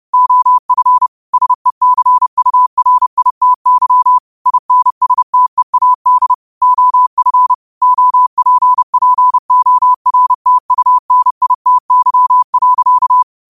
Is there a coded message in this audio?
Yes: Morse code from 0.1 to 13.2 s, heard as a keyed tone, about 30 wpm, 1000 Hz. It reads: OF SECURITY INSTEAD OF OPPORTUNITY.